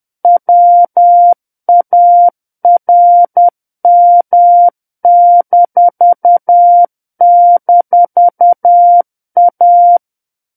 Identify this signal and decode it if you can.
Morse code, heard as a keyed tone, about 10 wpm, 700 Hz. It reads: WARM--A
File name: Book One / 01-Chapter 1 - 10 WPM